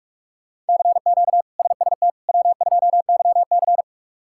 Morse code, heard as a keyed tone, about 44 wpm, 700 Hz. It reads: CQ SST W2YC
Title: K1USN SST QSO Practice - 45wpm